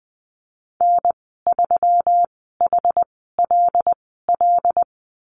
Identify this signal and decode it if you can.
Morse code, heard as a keyed tone, about 20 wpm, spaced as for 15 wpm, 700 Hz. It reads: N3HLL
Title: 128 Lesson 21g - Number 6 - Call Signs 20wpm